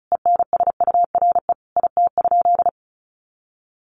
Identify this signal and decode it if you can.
Morse code, heard as a keyed tone, about 34 wpm, 700 Hz. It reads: ENSURE IT?
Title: Sentences from Top 1100 Words 35wpm